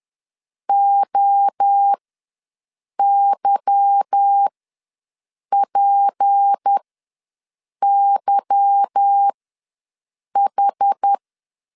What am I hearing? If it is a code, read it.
Morse code, heard as a keyed tone, about 10 wpm, spaced as for 6 wpm, 800 Hz. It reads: OYPYH
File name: LEZIONI / 8 YJ45 / 8 Y ws 10wpm 3952